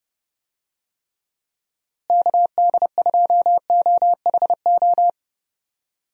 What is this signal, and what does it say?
Morse code, heard as a keyed tone, about 30 wpm, 700 Hz. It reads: KD2OHO
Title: USA Call Signs - 6 Characters - Repeated 2x - 30wpm